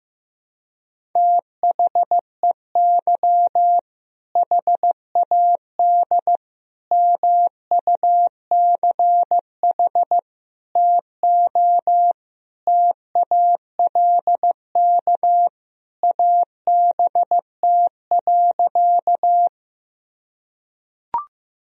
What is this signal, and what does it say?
Morse code, heard as a keyed tone, about 15 wpm, 700 Hz. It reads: THEY HAD MUCH TO TALK ABT.
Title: Sentences from Top 400 Words and Abbreviations 15wpm